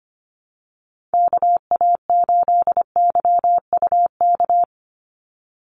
Morse code, heard as a keyed tone, about 25 wpm, 700 Hz. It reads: KA8YUK
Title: USA Call Signs 25wpm